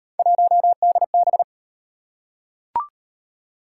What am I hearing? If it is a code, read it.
Morse code, heard as a keyed tone, about 38 wpm, 700 Hz. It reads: 1DB